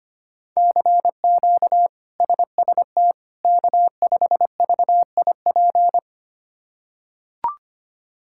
Morse code, heard as a keyed tone, about 25 wpm, 700 Hz. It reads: CQ SST K5VIP
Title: K1USN SST QSO Practice - 25wpm